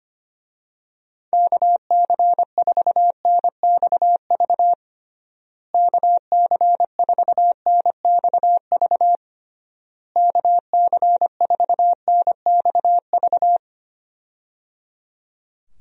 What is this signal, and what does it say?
Morse code, heard as a keyed tone, about 25 wpm, 700 Hz. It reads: KC4NXV KC4NXV KC4NXV
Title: USA Call Signs - 6 Characters - Repeated 3x - 25wpm